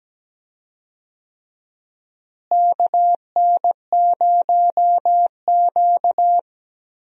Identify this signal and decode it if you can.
Morse code, heard as a keyed tone, about 17 wpm, 700 Hz. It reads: KN0Q
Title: USA Call Signs - 4 Characters - Repeated 3x - 17wpm